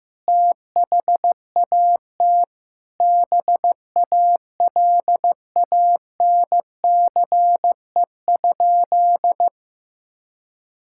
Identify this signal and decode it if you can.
Morse code, heard as a keyed tone, about 15 wpm, 700 Hz. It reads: THAT BALANCE?
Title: Sentences from Top 1000 Words - Review Part 6 - 15wpm